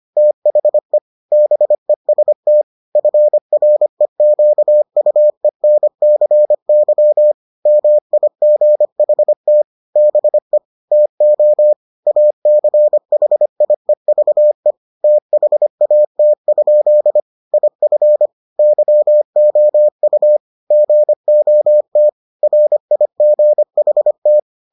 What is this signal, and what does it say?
Morse code, heard as a keyed tone, about 25 wpm, 600 Hz. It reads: THE BEST FREQUENCY MIGHT BE TO ACHIEVE THAT? IF YOU GOT RIGHT